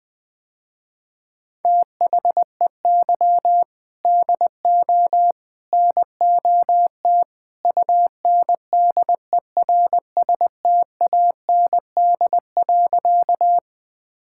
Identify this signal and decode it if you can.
Morse code, heard as a keyed tone, about 20 wpm, 700 Hz. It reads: THEY DO NOT UNDERSTAND.
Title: Sentences with 3-Word Phrases - Spoken Beginning - 20wpm